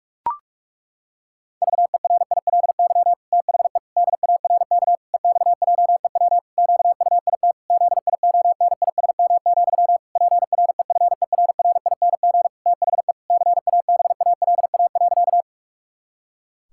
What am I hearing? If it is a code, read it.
Morse code, heard as a keyed tone, about 41 wpm, 700 Hz. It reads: VERILY THE DARK EXJEW QUIT ZIONISM, PREFERRING THE CABALA.
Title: Pangram Challenge v2 - 45wpm